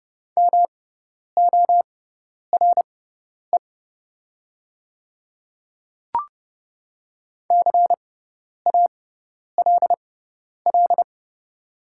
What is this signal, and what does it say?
Morse code, heard as a keyed tone, about 30 wpm, spaced as for 10 wpm, 700 Hz. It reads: MORE CALL
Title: 6x Character Spacing - Top 100 Words - 30wpm